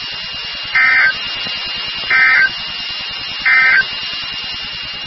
Emergency Alert System (SAME) tones, heard as harsh buzzing bursts. It is heard from 0.6 to 3.9 s.
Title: NOAA Weather Radio WXK36